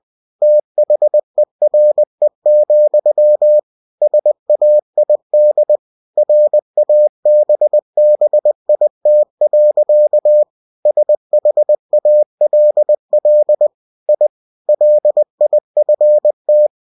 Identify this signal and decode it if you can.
Morse code, heard as a keyed tone, about 20 wpm, 600 Hz. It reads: THERE, SAID RABBIT. SHALL I LIFT
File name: cw-books / winnie-the-pooh / winnie-20wpm-page-0069